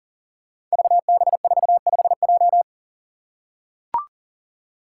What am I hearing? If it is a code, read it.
Morse code, heard as a keyed tone, about 40 wpm, 700 Hz. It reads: VB45J
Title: Code Groups - 5 Letter:Numbers - 40wpm